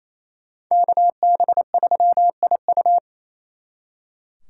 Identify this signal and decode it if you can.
Morse code, heard as a keyed tone, about 36 wpm, 700 Hz. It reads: KB3IU